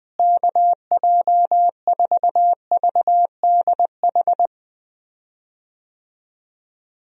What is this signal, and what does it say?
Morse code, heard as a keyed tone, about 20 wpm, 700 Hz. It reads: KJ4VDH
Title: USA Call Signs - 6 Characters - Repeated 2x - 20wpm